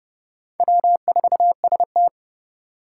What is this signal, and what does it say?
Morse code, heard as a keyed tone, about 30 wpm, 700 Hz. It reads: W4ST
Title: Four Challenging Letter-Number Combinations 30wpm